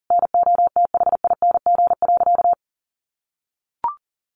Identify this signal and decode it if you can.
Morse code, heard as a keyed tone, about 37 wpm, 700 Hz. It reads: NOTHING.
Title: Sentences from Top 900 Words 40wpm